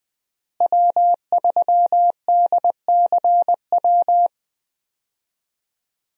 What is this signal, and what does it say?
Morse code, heard as a keyed tone, about 20 wpm, 700 Hz. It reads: W3DCW
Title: USA Call Signs - 5 Characters - Repeated 3x - 20wpm